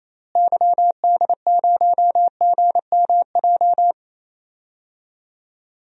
Morse code, heard as a keyed tone, about 29 wpm, 700 Hz. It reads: YD0GMJ
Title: DXCC-100 Call Signs - 6 Characters - Repeated 2x - 28wpm